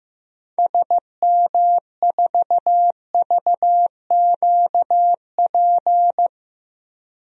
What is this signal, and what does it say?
Morse code, heard as a keyed tone, about 15 wpm, 700 Hz. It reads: SM4VQP
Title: DXCC-100 Call Signs - 6 Characters - 15wpm